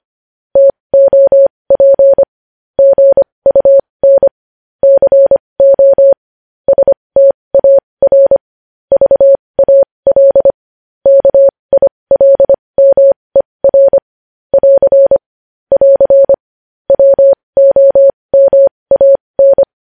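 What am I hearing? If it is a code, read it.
Morse code, heard as a keyed tone, about 25 wpm, spaced as for 20 wpm, 550 Hz. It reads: TOP GUN CO STAR VAL KILMER <AR> <AR> WOMAN
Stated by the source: Morse code news headlines practice at 20 wpm.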